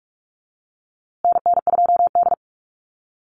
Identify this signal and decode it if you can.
Morse code, heard as a keyed tone, about 44 wpm, 700 Hz. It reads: NN2D
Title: USA Call Signs Repeated 2x 45wpm